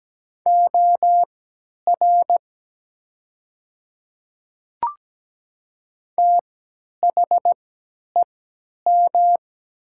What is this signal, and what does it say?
Morse code, heard as a keyed tone, about 17 wpm, spaced as for 10 wpm, 700 Hz. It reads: OR THEM
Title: 3x Character Spacing - Top 200 Words - 17wpm